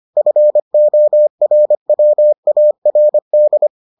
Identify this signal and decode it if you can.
Morse code, heard as a keyed tone, about 25 wpm, 600 Hz. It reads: FORWARD